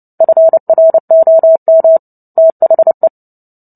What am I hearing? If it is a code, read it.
Morse code, heard as a keyed tone, about 30 wpm, 650 Hz. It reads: FROM THE
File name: Book Two / 03-Chapter 3 - 29 WPM